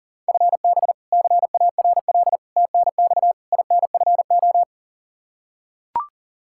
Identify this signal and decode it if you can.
Morse code, heard as a keyed tone, about 37 wpm, 700 Hz. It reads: FB CARL TNX INFO
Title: Practice QSO - Exchange 3 - More Info - 40wpm